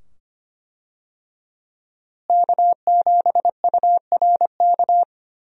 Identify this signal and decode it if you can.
Morse code, heard as a keyed tone, about 25 wpm, 700 Hz. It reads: K7URK
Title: USA Call Signs - 5 Characters - Repeated 2x - 25wpm